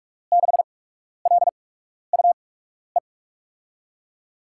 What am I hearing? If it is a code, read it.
Morse code, heard as a keyed tone, about 46 wpm, spaced as for 12 wpm, 700 Hz. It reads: 6LUE
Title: USA Call Signs - 6 Characters - 8x Character Spacing - 45wpm